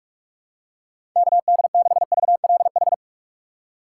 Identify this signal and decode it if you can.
Morse code, heard as a keyed tone, about 44 wpm, 700 Hz. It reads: KD6VLH
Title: USA Call Signs - 6 Characters - 2x Character Spacing - 45wpm